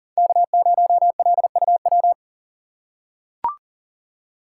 Morse code, heard as a keyed tone, about 40 wpm, 690 Hz. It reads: K0LUW
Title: K1USN SST QSO Practice - 40wpm